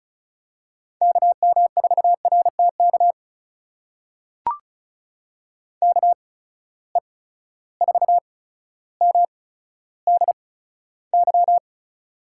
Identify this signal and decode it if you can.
Morse code, heard as a keyed tone, about 34 wpm, spaced as for 15 wpm, 700 Hz. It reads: KM4RTK KE4MDY
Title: USA Call Signs - 6 Characters - 8x Character Spacing - 35wpm